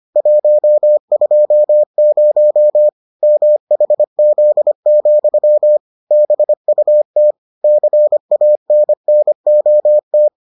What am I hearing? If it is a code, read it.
Morse code, heard as a keyed tone, about 25 wpm, 600 Hz. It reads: 120 MHZ, BUT CANNOT